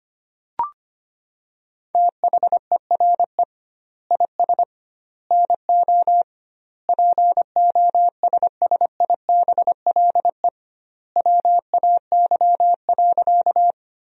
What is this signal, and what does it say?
Morse code, heard as a keyed tone, about 25 wpm, 700 Hz. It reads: THERE IS NO POSSIBLE WAY.
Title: 2x Word-Spacing - Sentences from Top 500 Words - 25wpm